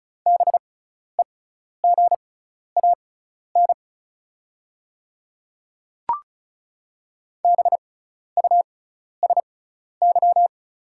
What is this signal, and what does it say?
Morse code, heard as a keyed tone, about 34 wpm, spaced as for 12 wpm, 690 Hz. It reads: BEGAN BUSY
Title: 6x Character Spacing - Top 500 Words - 35wpm